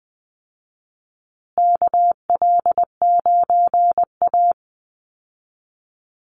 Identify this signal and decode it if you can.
Morse code, heard as a keyed tone, about 20 wpm, 700 Hz. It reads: KL9A
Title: Call Signs Repeated 3x - 20wpm